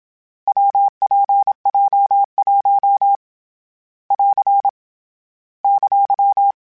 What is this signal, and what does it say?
Morse code, heard as a keyed tone, about 27 wpm, 800 Hz. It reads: WPJ1 <AR> !
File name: Variable Speed / Variable Speed-mp3s / WPJ WPJ1 sp @25